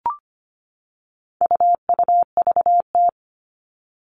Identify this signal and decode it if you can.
Morse code, heard as a keyed tone, about 25 wpm, 700 Hz. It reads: UUVT
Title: All Cut Numbers - 4-Digit 25wpm